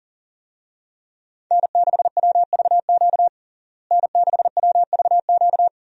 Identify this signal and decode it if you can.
Morse code, heard as a keyed tone, about 40 wpm, 700 Hz. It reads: N6WVQ N6WVQ
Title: USA Call Signs - 5 Characters - Repeated 3x - 40wpm